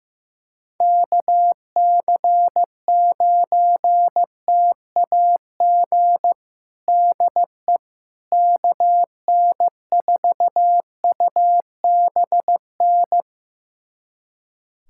Morse code, heard as a keyed tone, about 15 wpm, 700 Hz. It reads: KC9TAG DE KN4UBN
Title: USA Callsign DE Callsign - 6 Characters - 15wpm